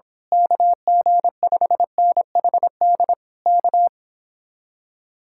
Morse code, heard as a keyed tone, about 26 wpm, 700 Hz. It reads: KG5NHD K
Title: Speed-Racing Common QSO Phrases - 22wpm